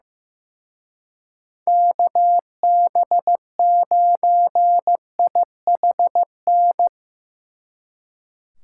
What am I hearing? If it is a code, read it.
Morse code, heard as a keyed tone, about 17 wpm, 700 Hz. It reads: KB9IHN
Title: Speed Racing - USA 2x3 Call Signs 15wpm